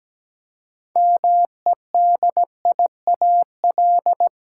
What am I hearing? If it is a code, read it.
Morse code, heard as a keyed tone, about 17 wpm, 700 Hz. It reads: MEDIAL
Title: Practice QSO - Exchange 3 - More Info - 17wpm